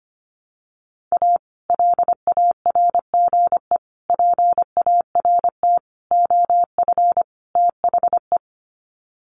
Morse code, heard as a keyed tone, about 25 wpm, 700 Hz. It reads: A LARGE PART OF THE
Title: Sets of 5 Words Encore2 25wpm